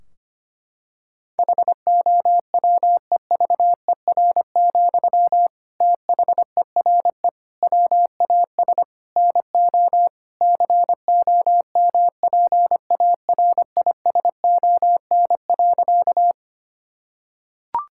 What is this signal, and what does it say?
Morse code, heard as a keyed tone, about 25 wpm, 700 Hz. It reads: HOWEVER, THERE WAS NO COMPARISON.
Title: Sentences from Top 1600 Words 25wpm